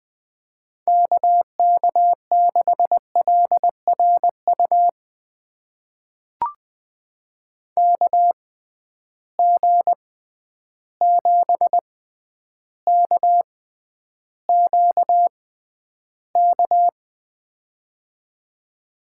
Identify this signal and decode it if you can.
Morse code, heard as a keyed tone, about 21 wpm, spaced as for 10 wpm, 700 Hz. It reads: KK6LRU KG7KQK
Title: USA Call Signs - 6 Characters - 6x Character Spacing - 20wpm